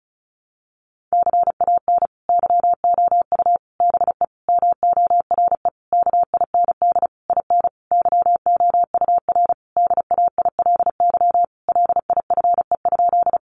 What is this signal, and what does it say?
Morse code, heard as a keyed tone, about 34 wpm, 700 Hz. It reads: CAN YOU BE MORE KIND IN YOUR DAILY LIFE?
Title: Sentences from Top 1100 Words 35wpm